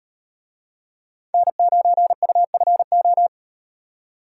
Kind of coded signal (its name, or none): Morse code